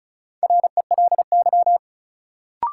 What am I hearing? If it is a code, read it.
Morse code, heard as a keyed tone, about 34 wpm, 700 Hz. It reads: RELY
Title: Sentences from Top 1200 Words - New Word Drill - 35wpm